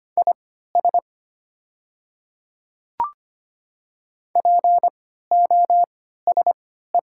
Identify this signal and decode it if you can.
Morse code, heard as a keyed tone, about 25 wpm, spaced as for 14 wpm, 700 Hz. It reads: IS POSE
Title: 3x Character Spacing - Top 500 Words - 25wpm